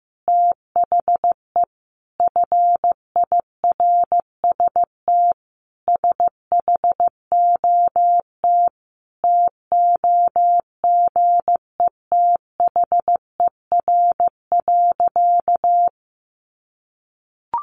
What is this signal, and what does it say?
Morse code, heard as a keyed tone, about 15 wpm, 700 Hz. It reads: THE FIRST SHOT TOGETHER.
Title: Sentences from Top 1100 Words 15wpm